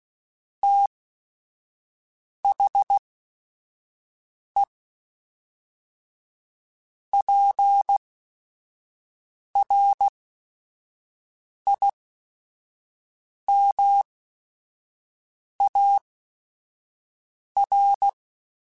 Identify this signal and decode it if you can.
Morse code, heard as a keyed tone, about 16 wpm, spaced as for 5 wpm, 800 Hz. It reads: THE PRIMAR
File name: Technical Subjects / Technical Subjects-mp3s / Sw Pwr Sup @5